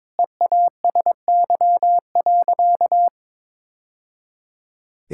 Morse code, heard as a keyed tone, about 21 wpm, 700 Hz. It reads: EASY.
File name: practice / 22wpm / Sentences with 2-Word Phrases - 22wpm